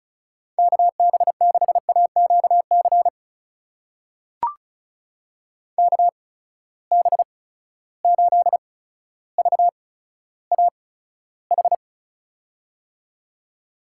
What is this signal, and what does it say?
Morse code, heard as a keyed tone, about 36 wpm, spaced as for 15 wpm, 700 Hz. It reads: KB6AQC KB8VAH
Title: USA Call Signs - 6 Characters - 8x Character Spacing - 35wpm